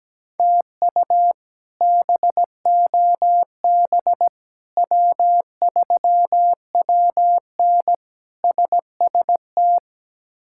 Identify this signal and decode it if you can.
Morse code, heard as a keyed tone, about 17 wpm, 700 Hz. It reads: TU BOB W3WN SST
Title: K1USN SST QSO Practice - 17wpm